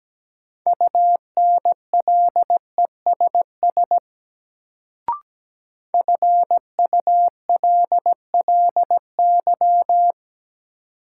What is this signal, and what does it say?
Morse code, heard as a keyed tone, about 17 wpm, 700 Hz. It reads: UNLESS FULLY
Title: Sentences from Top 1300 Words - New Word Drill Alt - 17wpm